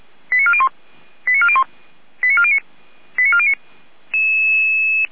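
Five-tone selective call (ZVEI), heard as quick runs of five tones, tones of about 70 ms. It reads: 89401 89401 89409 89409
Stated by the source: ZVEI five-tone selective call received on SDR.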